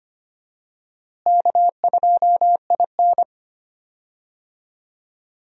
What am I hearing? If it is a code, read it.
Morse code, heard as a keyed tone, about 25 wpm, 700 Hz. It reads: K2IN
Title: USA Call Signs - 4 Characters - Repeated 2x - 25wpm